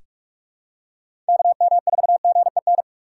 Morse code, heard as a keyed tone, about 46 wpm, 700 Hz. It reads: KM4GEN
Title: USA Call Signs - 6 Characters - 2x Character Spacing - 45wpm